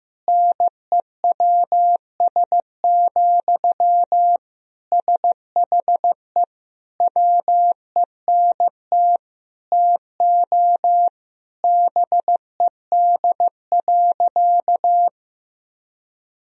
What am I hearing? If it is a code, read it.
Morse code, heard as a keyed tone, about 15 wpm, 700 Hz. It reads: NEWS, SHE WENT TO BED.
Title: Sentences from Top 1000 Words - Review Part 3 - 15wpm